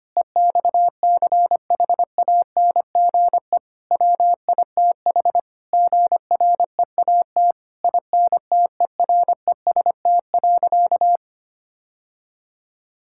Morse code, heard as a keyed tone, about 25 wpm, 700 Hz. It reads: EXCHANGE WITH GREAT INTEREST.
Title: Sentences from Top 1000 Words - Review Part 6 - 25wpm